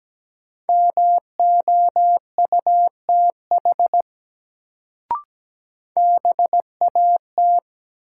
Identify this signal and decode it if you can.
Morse code, heard as a keyed tone, about 17 wpm, 700 Hz. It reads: MOUTH BAT
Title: Sentences from Top 1500 Words - New Word Drill Alt - 17wpm